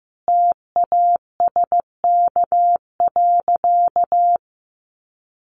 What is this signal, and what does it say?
Morse code, heard as a keyed tone, about 15 wpm, 700 Hz. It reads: TASK.
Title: Sentences from Top 1300 Words 15wpm